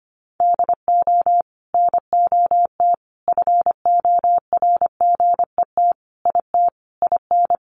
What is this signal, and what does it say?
Morse code, heard as a keyed tone, about 25 wpm, 700 Hz. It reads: DO NOT FORGET IT IN
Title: Sentences from Top 1200 Words 25wpm